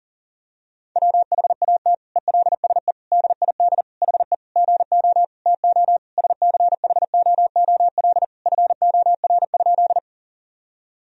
Morse code, heard as a keyed tone, about 40 wpm, 700 Hz. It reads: WHAT ELSE DID HE GO TO SCHOOL FOR?
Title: Sentences from Top 1000 Words - Review Part 1 - 40wpm